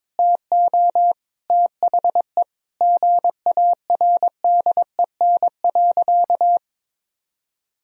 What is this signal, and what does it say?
Morse code, heard as a keyed tone, about 22 wpm, 700 Hz. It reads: TO THE GARDEN.
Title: Sentences from Top 1000 Words - Review Part 4 - 22wpm